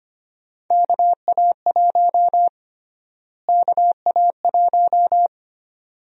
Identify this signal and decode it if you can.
Morse code, heard as a keyed tone, about 25 wpm, 700 Hz. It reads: KA1 KA1
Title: USA Call Sign Prefixes - Repeated 2x - 25wpm